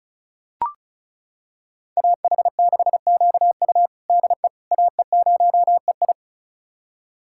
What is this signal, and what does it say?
Morse code, heard as a keyed tone, about 34 wpm, 700 Hz. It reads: AH6QU DE AE0EI
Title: USA Callsign DE Callsign - 5 Characters - 35wpm